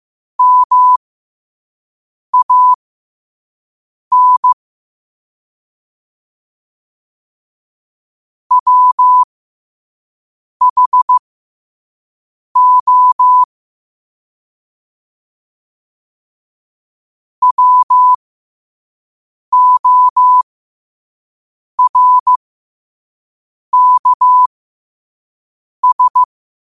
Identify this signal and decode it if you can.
Morse code, heard as a keyed tone, about 15 wpm, spaced as for 5 wpm, 1000 Hz. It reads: MAN WHO WORKS
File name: qotdfiles / 5wpm